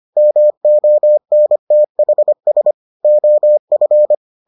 Morse code, heard as a keyed tone, about 25 wpm, 600 Hz. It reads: MONTHS OF